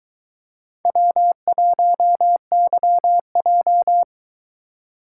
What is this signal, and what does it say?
Morse code, heard as a keyed tone, about 22 wpm, 700 Hz. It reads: W1YJ